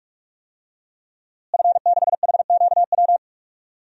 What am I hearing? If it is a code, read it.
Morse code, heard as a keyed tone, about 43 wpm, 700 Hz. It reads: F6HQW
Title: DXCC-100 Call Signs - 5 Characters - Repeated 2x - 45wpm